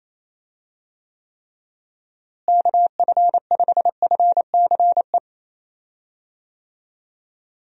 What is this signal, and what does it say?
Morse code, heard as a keyed tone, about 27 wpm, 700 Hz. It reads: KF5FCE
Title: USA Call Signs - 6 Characters - Repeated 2x - 28wpm